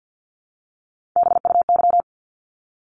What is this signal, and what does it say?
Morse code, heard as a keyed tone, about 50 wpm, 700 Hz. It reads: BUY
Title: Top 501-750 Words 50wpm